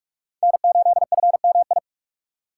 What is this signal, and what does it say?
Morse code, heard as a keyed tone, about 46 wpm, 700 Hz. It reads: N8FMI
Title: USA Call Signs - 5 Characters - 3x Character Spacing - 45wpm